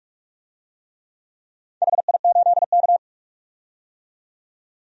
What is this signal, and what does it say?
Morse code, heard as a keyed tone, about 41 wpm, 700 Hz. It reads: HI8K
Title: DXCC-100 Call Signs - 4 Characters - 45wpm